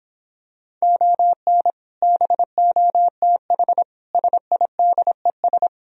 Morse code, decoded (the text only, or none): ON BOTH SIDES